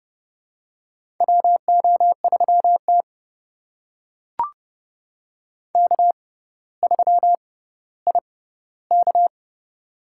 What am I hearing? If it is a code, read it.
Morse code, heard as a keyed tone, about 30 wpm, spaced as for 15 wpm, 700 Hz. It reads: WO3T K3IK
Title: USA Call Signs - 4 Characters - 6x Character Spacing - 30wpm